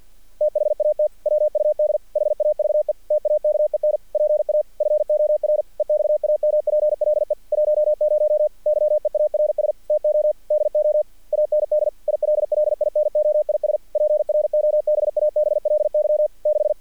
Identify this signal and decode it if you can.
Morse code, heard as a keyed tone, about 48 wpm, 600 Hz. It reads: THAT WUD HAVE TAKEN PA FOR EXAMPLE 10 YEARS TO DO AND ILLINOIS PROBABLY 6
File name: archived / CRSnet